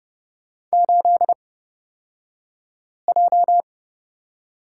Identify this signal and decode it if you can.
Morse code, heard as a keyed tone, about 29 wpm, spaced as for 5 wpm, 700 Hz. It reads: 8J
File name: Lesson 36d - BT - Two Characters 30wpm